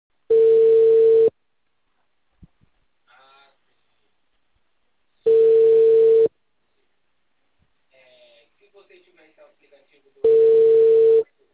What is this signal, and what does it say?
Ringback tone (China, 450 Hz), with speech.